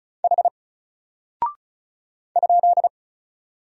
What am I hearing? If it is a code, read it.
Morse code, heard as a keyed tone, about 36 wpm, 700 Hz. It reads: H ?
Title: Single Character (Letter:Number:Punctuation:Prosign) Alt 35wpm